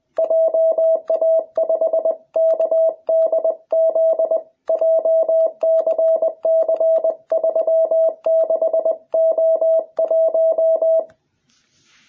Morse code, heard as a keyed tone, about 20 wpm, 660 Hz. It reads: JA5KDZ JCC36O1